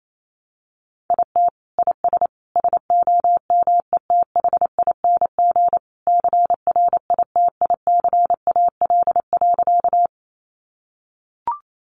Morse code, heard as a keyed tone, about 28 wpm, 700 Hz. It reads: IT IS SOMETHING CRITICAL.
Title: Sentences from Top 1100 Words 28wpm